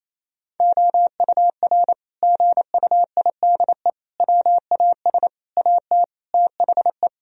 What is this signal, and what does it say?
Morse code, heard as a keyed tone, about 28 wpm, 700 Hz. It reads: OUR GUIDE WAS AT THE
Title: Sentences from Top 1000 Words - Review Part 3 - 28wpm